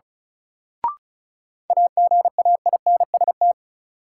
Morse code, heard as a keyed tone, about 34 wpm, 700 Hz. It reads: AGAINST
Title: Sentences from Top 400 Words - New Word Drill - 35wpm